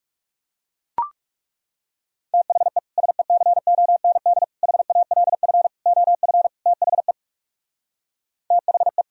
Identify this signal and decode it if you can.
Morse code, heard as a keyed tone, about 40 wpm, 700 Hz. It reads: THE SECOND HALF OF THE THE